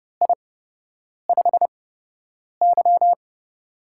Morse code, heard as a keyed tone, about 31 wpm, spaced as for 8 wpm, 700 Hz. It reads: I5Y
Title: USA Call Signs - 4 Characters - 8x Character Spacing - 30wpm